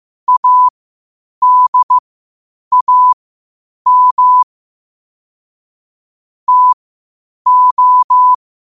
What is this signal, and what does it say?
Morse code, heard as a keyed tone, about 15 wpm, spaced as for 8 wpm, 1000 Hz. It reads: ADAM TO
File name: qotdfiles / 8wpm